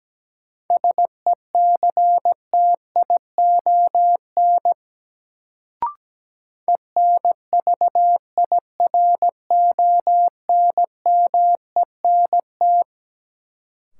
Morse code, heard as a keyed tone, about 17 wpm, 700 Hz. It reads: SECTION ENVIRONMENT
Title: Sentences from Top 1000 Words - Review Part 5 - New Word Drill - 17wpm